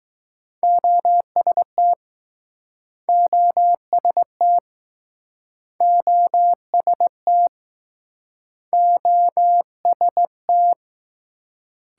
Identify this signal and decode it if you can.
Morse code, heard as a keyed tone, about 21 wpm, 700 Hz. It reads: OST OST OST OST